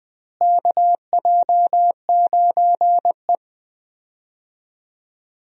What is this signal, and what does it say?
Morse code, heard as a keyed tone, about 20 wpm, 700 Hz. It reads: KJ9E
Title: USA Call Signs - 4 Characters - Repeated 3x - 20wpm